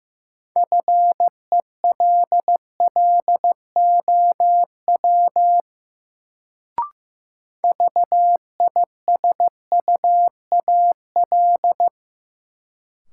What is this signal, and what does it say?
Morse code, heard as a keyed tone, about 15 wpm, 700 Hz. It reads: FELLOW VISUAL
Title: Sets of 5 Words Encore - New Word Drill - 15wpm